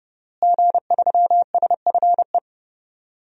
Morse code, heard as a keyed tone, about 31 wpm, 700 Hz. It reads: G3SFE
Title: DXCC-100 Call Signs - 5 Characters - Repeated 2x - 30wpm